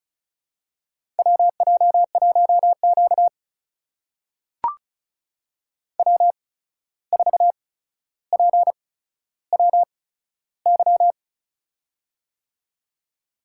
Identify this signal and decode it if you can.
Morse code, heard as a keyed tone, about 36 wpm, spaced as for 14 wpm, 700 Hz. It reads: WJ1Q W4PWY
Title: USA Call Signs - 8x Character Spacing - 35wpm